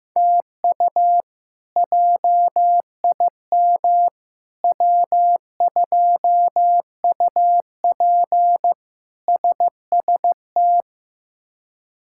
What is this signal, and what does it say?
Morse code, heard as a keyed tone, about 15 wpm, 700 Hz. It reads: TU JIM W2UP SST
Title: K1USN SST QSO Practice - 15wpm